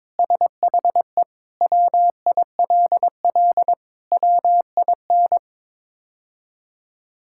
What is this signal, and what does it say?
Morse code, heard as a keyed tone, about 22 wpm, 700 Hz. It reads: SHE WILL WIN
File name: practice / 22wpm / Sentences with 3-Word Phrases - 22wpm